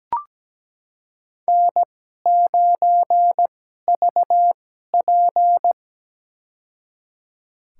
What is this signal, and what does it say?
Morse code, heard as a keyed tone, about 17 wpm, spaced as for 12 wpm, 700 Hz. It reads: N9VP
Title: USA Call Signs - 4 Characters - 2x Character Spacing - 17wpm